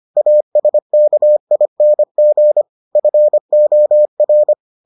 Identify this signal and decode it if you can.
Morse code, heard as a keyed tone, about 25 wpm, 600 Hz. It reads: ASKING FOR